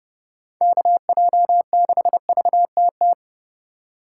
Morse code, heard as a keyed tone, about 28 wpm, 700 Hz. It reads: KJ6VTT